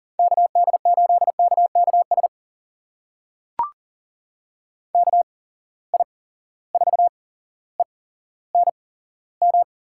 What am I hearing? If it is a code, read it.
Morse code, heard as a keyed tone, about 38 wpm, spaced as for 17 wpm, 700 Hz. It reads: KD8KKS KI4ENM
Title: USA Call Signs - 6 Characters - 8x Character Spacing - 40wpm